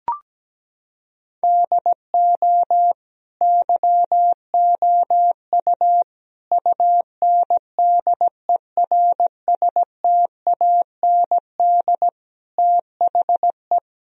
Morse code, heard as a keyed tone, about 17 wpm, 700 Hz. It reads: DO YOU UNDERSTAND THE
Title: Sentences from Top 1000 Words - Review Part 3 - 17wpm